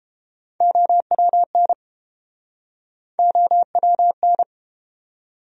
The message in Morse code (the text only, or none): OWN OWN